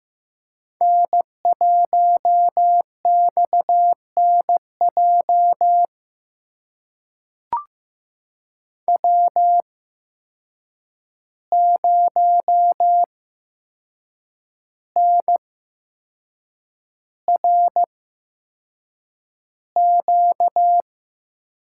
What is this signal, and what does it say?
Morse code, heard as a keyed tone, about 15 wpm, spaced as for 6 wpm, 700 Hz. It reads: N1XNJ W0NRQ
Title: USA Call Signs - 5 Characters - 8x Character Spacing - 15wpm